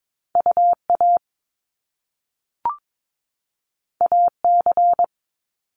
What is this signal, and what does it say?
Morse code, heard as a keyed tone, about 22 wpm, 700 Hz. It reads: UA AC
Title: Top Two Letter Combinations in Words 22wpm